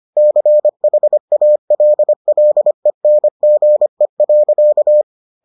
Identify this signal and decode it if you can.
Morse code, heard as a keyed tone, about 25 wpm, 600 Hz. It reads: CHALLENGE.